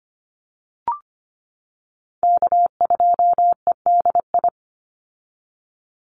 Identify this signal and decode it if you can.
Morse code, heard as a keyed tone, about 25 wpm, 700 Hz. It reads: K2EDI
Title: USA Call Signs 25wpm